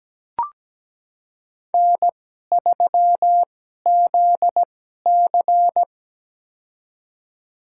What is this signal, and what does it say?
Morse code, heard as a keyed tone, about 17 wpm, spaced as for 12 wpm, 700 Hz. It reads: N3ZC
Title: USA Call Signs - 4 Characters - 2x Character Spacing - 17wpm